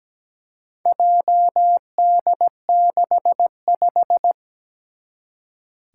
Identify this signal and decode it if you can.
Morse code, heard as a keyed tone, about 17 wpm, 700 Hz. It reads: JD65
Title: Four Challenging Letter-Number Combinations 17wpm